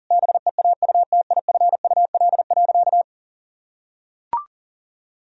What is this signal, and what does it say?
Morse code, heard as a keyed tone, about 40 wpm, 700 Hz. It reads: BEAUTIFUL.
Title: Sentences from Top 1000 Words - Review Part 4 - 40wpm